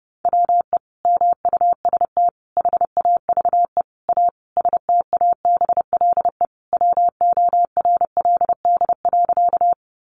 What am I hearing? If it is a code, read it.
Morse code, heard as a keyed tone, about 30 wpm, 700 Hz. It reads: WE MUST HAVE A STABLE WORLD.
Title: Sentences from Top 1500 Words 30wpm